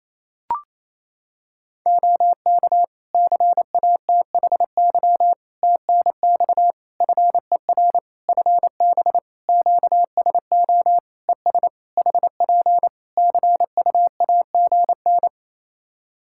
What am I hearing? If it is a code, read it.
Morse code, heard as a keyed tone, about 28 wpm, 700 Hz. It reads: OK CATHY TNX FER FB QSO ES HP CUAGN
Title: K1USN SST QSO Practice 28wpm